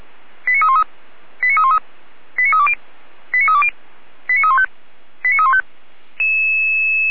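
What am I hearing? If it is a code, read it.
Five-tone selective call (ZVEI), heard as quick runs of five tones, tones of about 70 ms. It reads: 89313 89313 89320 89320 89315 89315
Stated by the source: ZVEI five-tone selective call received with an SDR, archived by the Orlamünde fire brigade (feuerwehr-orlamuende).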